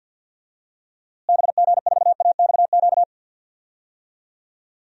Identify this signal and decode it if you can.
Morse code, heard as a keyed tone, about 50 wpm, 700 Hz. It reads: BG4AXQ
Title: DXCC-100 Call Signs - 6 Characters - Repeated 2x - 50wpm